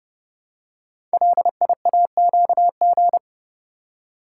Morse code, heard as a keyed tone, about 30 wpm, 700 Hz. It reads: LIAQG
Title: Code Groups - 5 Letters - 30wpm